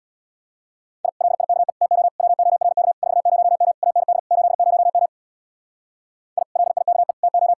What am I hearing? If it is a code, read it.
Morse code, heard as a keyed tone, about 64 wpm, 700 Hz. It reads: I LIKE TO WORK 40M AND 80M I LIKE TO